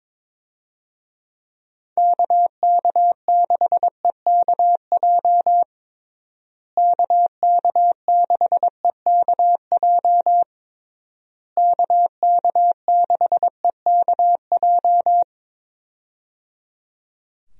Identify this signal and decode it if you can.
Morse code, heard as a keyed tone, about 22 wpm, 700 Hz. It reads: KK6EKJ KK6EKJ KK6EKJ
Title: USA Call Signs - 6 Characters - Repeated 3x - 22wpm